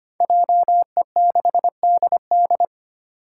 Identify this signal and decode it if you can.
Morse code, heard as a keyed tone, about 25 wpm, 700 Hz. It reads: JE6DD
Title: Five Challenging Letter-Number Combinations 25wpm